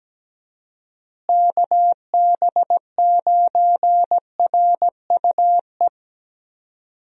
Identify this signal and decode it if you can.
Morse code, heard as a keyed tone, about 17 wpm, 700 Hz. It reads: KB9RUE
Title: USA Call Signs - 6 Characters - Repeated 3x - 17wpm